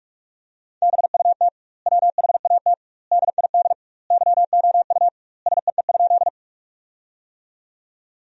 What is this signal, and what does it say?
Morse code, heard as a keyed tone, about 46 wpm, 700 Hz. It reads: BUT WHAT DID YOU SEE?
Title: 2x Word-Spacing - Sentences from Top 100 Words - 45wpm